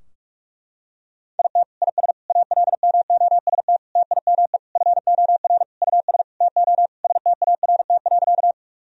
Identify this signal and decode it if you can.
Morse code, heard as a keyed tone, about 46 wpm, 700 Hz. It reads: IT IS ALMOST TIME FOR US TO START.
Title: Sentences from Top 1000 Words - Review Part 4 - 45wpm